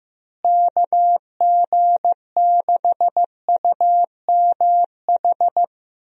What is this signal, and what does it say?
Morse code, heard as a keyed tone, about 15 wpm, 700 Hz. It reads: KG6UMH
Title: USA Call Signs - 6 Characters - Repeated 2x - 15wpm